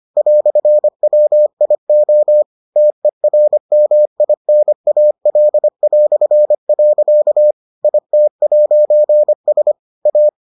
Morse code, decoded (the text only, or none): "WIO TERMINAL". IT'S A